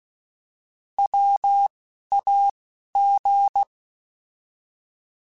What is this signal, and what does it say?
Morse code, heard as a keyed tone, about 16 wpm, 800 Hz. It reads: W A G
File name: Alphabetical / Alphabetical mp3s / Ws @10